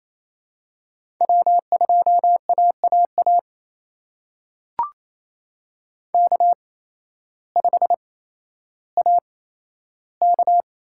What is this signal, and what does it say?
Morse code, heard as a keyed tone, about 28 wpm, spaced as for 13 wpm, 700 Hz. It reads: W2AAA K5AK
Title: USA Call Signs - 5 Characters - 8x Character Spacing - 28wpm